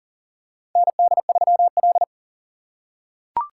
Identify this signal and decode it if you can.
Morse code, heard as a keyed tone, about 40 wpm, 700 Hz. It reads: ND3L
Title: K1USN SST QSO Practice - 40wpm